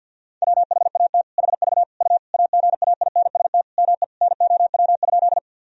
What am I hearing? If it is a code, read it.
Morse code, heard as a keyed tone, about 50 wpm, 700 Hz. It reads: WHAT HV U AGAINST ME NOW?
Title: Sentences from Top 400 Words and Abbreviations 50wpm